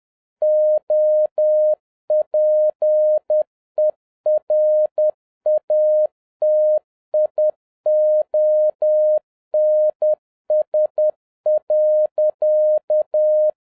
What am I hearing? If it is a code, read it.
Morse code, heard as a keyed tone, about 10 wpm, 610 Hz. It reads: OPERATIONS.